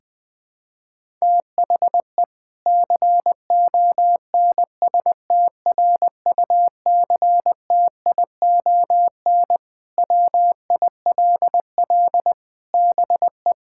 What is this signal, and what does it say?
Morse code, heard as a keyed tone, about 21 wpm, 700 Hz. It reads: THE CONSTRUCTION WILL BE
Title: Sentences from Top 1600 Words 20wpm